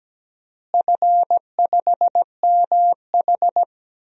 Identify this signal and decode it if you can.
Morse code, heard as a keyed tone, about 17 wpm, 700 Hz. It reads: F5MH
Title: DXCC-100 Call Signs - 4 Characters - Repeated 3x - 17wpm